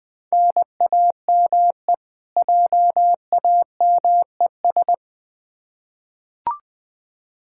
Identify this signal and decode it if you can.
Morse code, heard as a keyed tone, about 20 wpm, 700 Hz. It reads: NAME JAMES
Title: Practice QSO - Exchange 1 - Beginning - 20wpm